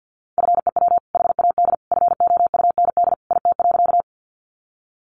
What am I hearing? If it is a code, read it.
Morse code, heard as a keyed tone, about 50 wpm, 700 Hz. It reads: FEW HAD FOUND IT.
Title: Sentences from Top 300 Words 50wpm